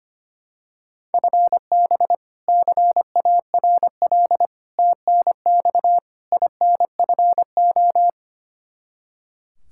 Morse code, heard as a keyed tone, about 25 wpm, 700 Hz. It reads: FB CARL TNX INFO
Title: Practice QSO - Exchange 3 - More Info - 25wpm